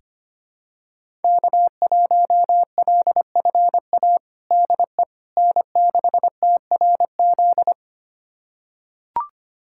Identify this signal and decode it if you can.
Morse code, heard as a keyed tone, about 25 wpm, 700 Hz. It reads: K1LFA DE N6TRZ
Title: USA Callsign DE Callsign - 5 Characters - 25wpm